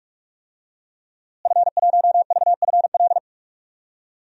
Morse code, heard as a keyed tone, about 46 wpm, 700 Hz. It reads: F1VFL
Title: DXCC-100 Call Signs - 5 Characters - Repeated 3x - 45wpm